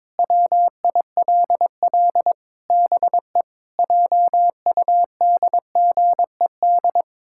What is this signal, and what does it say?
Morse code, heard as a keyed tone, about 22 wpm, 700 Hz. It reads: WILL BE JUDGED
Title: Pangram Challenge v2 - 22wpm